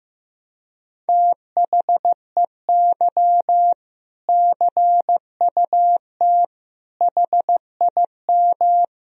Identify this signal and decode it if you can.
Morse code, heard as a keyed tone, about 15 wpm, 700 Hz. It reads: THEY CUT HIM
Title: Sentences from Top 1000 Words - Review Part 1 - 15wpm